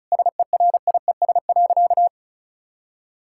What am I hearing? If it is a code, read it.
Morse code, heard as a keyed tone, about 36 wpm, 700 Hz. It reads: SERIES.
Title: Sentences from Top 1000 Words - Review Part 5 - 35wpm